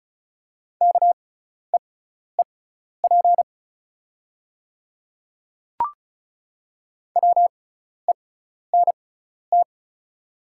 Morse code, heard as a keyed tone, about 36 wpm, spaced as for 12 wpm, 690 Hz. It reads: KEEP WENT
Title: 6x Character Spacing - Top 300 Words - 35wpm